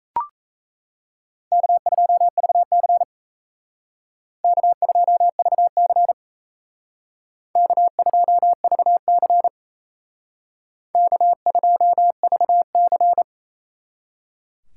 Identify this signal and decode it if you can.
Morse code, heard as a keyed tone, about 32 wpm, 700 Hz. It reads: K2VC K2VC K2VC K2VC
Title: Speed Racing - USA 1x2 Call Signs 28wpm